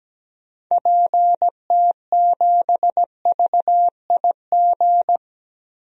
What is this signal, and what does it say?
Morse code, heard as a keyed tone, about 17 wpm, 700 Hz. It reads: PT7VIG
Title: DXCC-100 Call Signs - 6 Characters - Repeated 3x - 17wpm